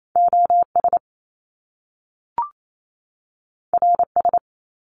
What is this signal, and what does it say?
Morse code, heard as a keyed tone, about 28 wpm, 700 Hz. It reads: OS RS
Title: Top Two Letter Combinations in Words 28wpm